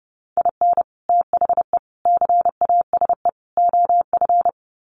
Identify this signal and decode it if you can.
Morse code, heard as a keyed tone, about 28 wpm, 700 Hz. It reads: IN THE CASE OF
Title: Sets of 4 Words 30wpm